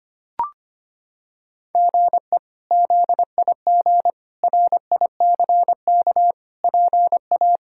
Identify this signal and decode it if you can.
Morse code, heard as a keyed tone, about 25 wpm, 700 Hz. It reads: GE ZIG RICK PA
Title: K1USN SST QSO Practice - 25wpm